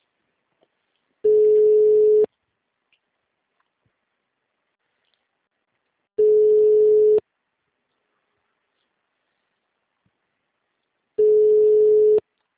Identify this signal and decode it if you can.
Ringback tone (Europe, 425 Hz).